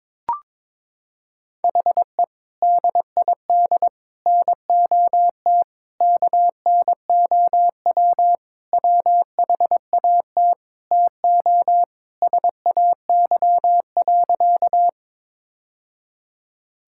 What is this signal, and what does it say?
Morse code, heard as a keyed tone, about 22 wpm, 700 Hz. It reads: HE DID NOT KNOW WHAT TO SAY.
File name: practice / 22wpm / Sentences with 3-Word Phrases - 22wpm